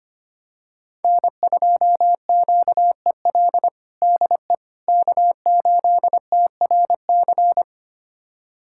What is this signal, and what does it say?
Morse code, heard as a keyed tone, about 25 wpm, 700 Hz. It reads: N2QEL DE K8TRC
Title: USA Callsign DE Callsign - 5 Characters - 25wpm